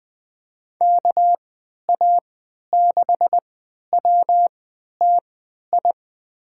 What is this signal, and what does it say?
Morse code, heard as a keyed tone, about 20 wpm, spaced as for 11 wpm, 700 Hz. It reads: KA6WTI
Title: USA Call Signs - 6 Characters - 3x Character Spacing - 20wpm